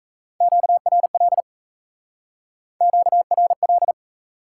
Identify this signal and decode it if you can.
Morse code, heard as a keyed tone, about 37 wpm, 700 Hz. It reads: QRL QRL